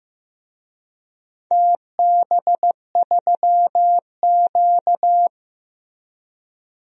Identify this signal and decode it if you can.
Morse code, heard as a keyed tone, about 15 wpm, 700 Hz. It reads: TB3Q
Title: DXCC-100 Call Signs - 4 Characters - Repeated 2x - 15wpm